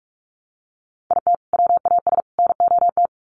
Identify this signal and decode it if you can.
Morse code, heard as a keyed tone, about 44 wpm, 700 Hz. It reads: IT WAS NOT
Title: Sets of 3 Words 45wpm